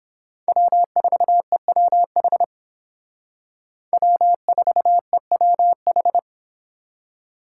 Morse code, heard as a keyed tone, about 31 wpm, 700 Hz. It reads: W4EWH W4EWH